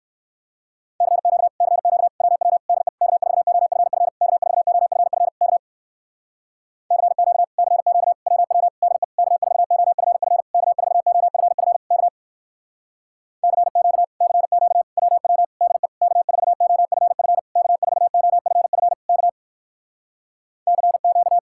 Morse code, heard as a keyed tone, about 61 wpm, 700 Hz. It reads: CQ CQ WW DE K4OFF K4OFF K CQ CQ WW DE K4OFF K4OFF K CQ CQ WW DE K4OFF K4OFF K CQ